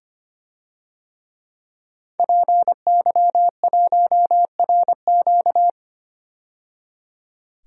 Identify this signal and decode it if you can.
Morse code, heard as a keyed tone, about 25 wpm, 700 Hz. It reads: PY1RQ
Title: DXCC-100 Call Signs - 5 Characters - 25wpm